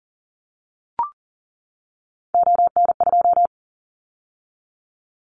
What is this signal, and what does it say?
Morse code, heard as a keyed tone, about 40 wpm, 700 Hz. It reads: ON2
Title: Callsign Prefixes for Top 100 DXCC - 40wpm